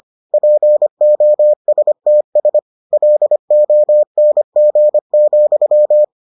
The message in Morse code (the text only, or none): POSTS LONG,